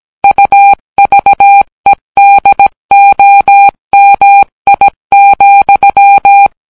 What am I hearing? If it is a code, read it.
Morse code, heard as a keyed tone, about 17 wpm, 800 Hz. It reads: UVEDOMI,